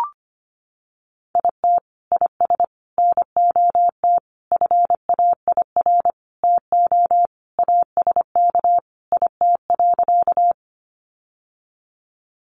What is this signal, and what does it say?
Morse code, heard as a keyed tone, about 25 wpm, 700 Hz. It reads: IT IS NOT FAIR TO ASK IT.
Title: Sentences from Top 1300 Words 25wpm